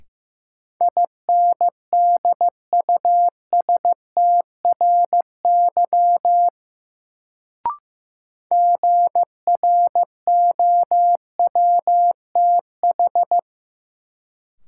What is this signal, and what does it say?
Morse code, heard as a keyed tone, about 16 wpm, 700 Hz. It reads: INDUSTRY GROWTH
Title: Sentences from Top 1300 Words - New Word Drill Alt - 15wpm